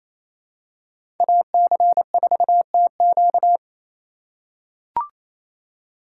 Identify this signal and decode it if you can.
Morse code, heard as a keyed tone, about 28 wpm, 700 Hz. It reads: AC4TQ
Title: USA Call Signs - 4x Character Spacing - 28wpm